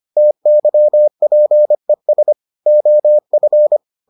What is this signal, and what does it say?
Morse code, heard as a keyed tone, about 25 wpm, 600 Hz. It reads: TYPES OF